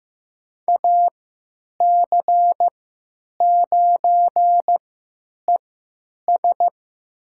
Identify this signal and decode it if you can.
Morse code, heard as a keyed tone, about 15 wpm, spaced as for 9 wpm, 700 Hz. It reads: AC9ES
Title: USA Call Signs - 5 Characters - 3x Character Spacing - 15wpm